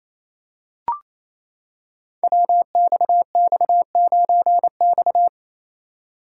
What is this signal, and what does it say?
Morse code, heard as a keyed tone, about 28 wpm, 690 Hz. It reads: WXX9X
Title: Code Groups - 5 Letter:Numbers - 28wpm